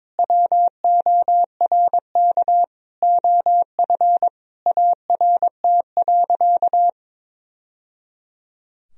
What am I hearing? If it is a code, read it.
Morse code, heard as a keyed tone, about 21 wpm, 700 Hz. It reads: WORK OF ART.
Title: Sentences from Top 1000 Words - Review Part 4 - 22wpm